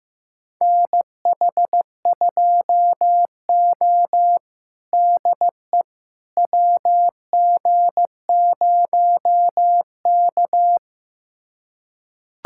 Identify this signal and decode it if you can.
Morse code, heard as a keyed tone, about 15 wpm, 700 Hz. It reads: NH2O DE WG0K
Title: USA Callsign DE Callsign - 4 Characters - 15wpm